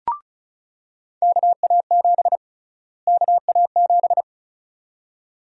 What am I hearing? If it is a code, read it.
Morse code, heard as a keyed tone, about 36 wpm, 700 Hz. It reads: KA7 KA7
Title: USA Call Sign Prefixes - Repeated 2x - 35wpm